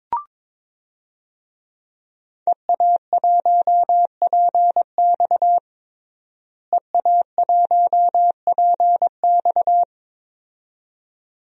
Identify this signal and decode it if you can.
Morse code, heard as a keyed tone, about 22 wpm, 700 Hz. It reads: EA1PX EA1PX
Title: DXCC-100 Call Signs - 5 Characters - Repeated 2x - 22wpm